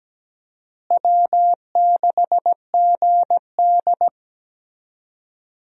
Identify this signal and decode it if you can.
Morse code, heard as a keyed tone, about 17 wpm, 700 Hz. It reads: W6GD
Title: USA Call Signs - 4 Characters - Repeated 2x - 17wpm